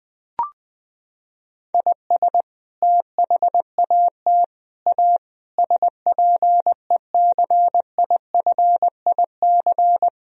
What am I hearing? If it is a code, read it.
Morse code, heard as a keyed tone, about 20 wpm, 700 Hz. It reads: IS THAT A SPECIFIC
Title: Sentences from Top 1000 Words - Review Part 3 - 20wpm